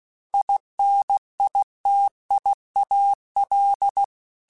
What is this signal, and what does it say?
Morse code, heard as a keyed tone, about 16 wpm, 800 Hz. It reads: INITIAL
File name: Technical Subjects / Technical Subjects-mp3s / Gates Intro @15